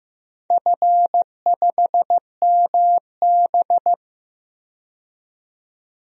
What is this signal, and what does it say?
Morse code, heard as a keyed tone, about 15 wpm, 700 Hz. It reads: F5MB
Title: DXCC-100 Call Signs - 4 Characters - 15wpm